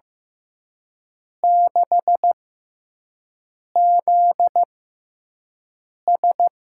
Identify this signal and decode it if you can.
Morse code, heard as a keyed tone, about 15 wpm, spaced as for 5 wpm, 700 Hz. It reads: 6ZS
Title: USA Call Signs - 4 Characters - 6x Character Spacing - 15wpm